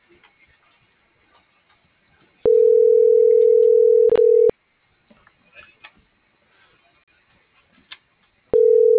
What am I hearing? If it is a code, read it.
Ringback tone (North America), 440 + 480 Hz.